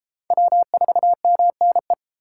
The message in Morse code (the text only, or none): W4MNE